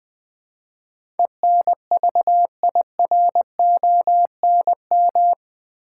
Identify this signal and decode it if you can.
Morse code, heard as a keyed tone, about 20 wpm, 700 Hz. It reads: ENVIRONM
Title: Build A Word - 4 Letters At A Time - 11-Letter Words - 20wpm